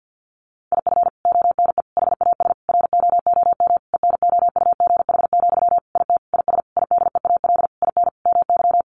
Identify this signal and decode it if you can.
Morse code, heard as a keyed tone, about 48 wpm, 700 Hz. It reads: IF ONE HAS ROOM ENOUGH, IT IS IDEAL IN MY
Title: Sentences from Top 1400 Words 50wpm